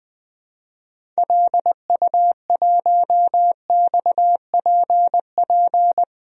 Morse code, heard as a keyed tone, about 20 wpm, 700 Hz. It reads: LU1XPP
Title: DXCC-100 Call Signs - 6 Characters - 20wpm